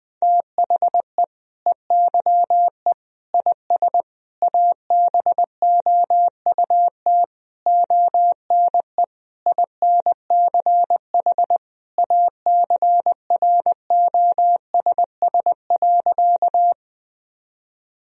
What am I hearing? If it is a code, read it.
Morse code, heard as a keyed tone, about 20 wpm, 700 Hz. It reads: THE EYE IS ABOUT ONE INCH ACROSS.
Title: Sentences from Top 1000 Words - Review Part 3 - 20wpm